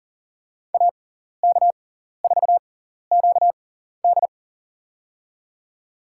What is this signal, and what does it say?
Morse code, heard as a keyed tone, about 40 wpm, spaced as for 14 wpm, 700 Hz. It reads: AK4QD
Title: USA Call Signs - 6x Character Spacing - 40wpm